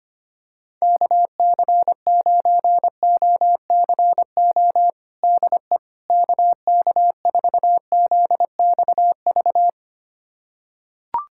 Morse code, heard as a keyed tone, about 25 wpm, 700 Hz. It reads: KC9OCO DE KK4ZXV
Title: USA Callsign DE Callsign 25wpm